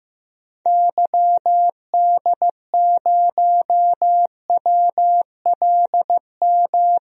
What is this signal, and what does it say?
Morse code, heard as a keyed tone, about 15 wpm, 700 Hz. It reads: YD0WLM
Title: DXCC-100 Call Signs - 6 Characters - 15wpm